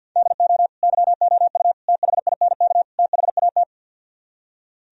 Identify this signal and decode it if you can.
Morse code, heard as a keyed tone, about 62 wpm, 700 Hz. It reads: DO YOU THINK THAT
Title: Speed-Racing - Sets of 4 Words 50wpm